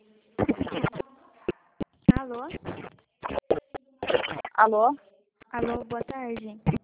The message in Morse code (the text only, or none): none